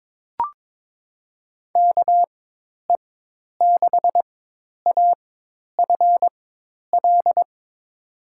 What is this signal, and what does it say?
Morse code, heard as a keyed tone, about 22 wpm, spaced as for 10 wpm, 700 Hz. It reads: KE6AFL
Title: USA Call Signs - 6 Characters - 4x Character Spacing - 22wpm